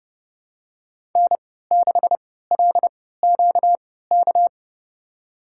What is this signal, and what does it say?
Morse code, heard as a keyed tone, about 30 wpm, spaced as for 17 wpm, 700 Hz. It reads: N6LQK
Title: USA Call Signs - 5 Characters - 3x Character Spacing - 30wpm